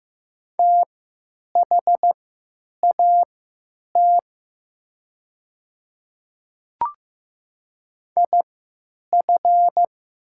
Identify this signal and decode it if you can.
Morse code, heard as a keyed tone, about 15 wpm, spaced as for 9 wpm, 700 Hz. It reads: THAT IF
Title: 3x Character Spacing - Top 100 Words - 15wpm